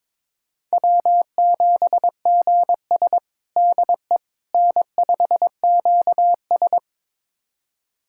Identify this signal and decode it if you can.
Morse code, heard as a keyed tone, about 22 wpm, 700 Hz. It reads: W7GS DE N5QS
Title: USA Callsign DE Callsign - 4 Characters - 22wpm